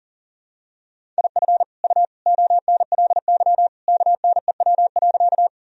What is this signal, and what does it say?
Morse code, heard as a keyed tone, about 40 wpm, 700 Hz. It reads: IF U ONLY KNEW.
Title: Sentences from Top 400 Words and Abbreviations 40wpm